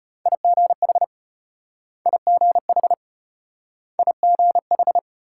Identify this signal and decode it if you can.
Morse code, heard as a keyed tone, about 33 wpm, 700 Hz. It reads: IGH IGH IGH